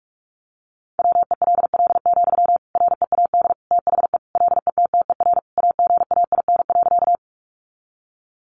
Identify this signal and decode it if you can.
Morse code, heard as a keyed tone, about 44 wpm, 700 Hz. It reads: WELL, READ THE LETTER AGAIN.
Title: Sentences from Top 300 Words 45wpm